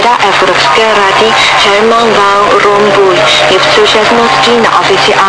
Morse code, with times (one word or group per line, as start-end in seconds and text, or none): none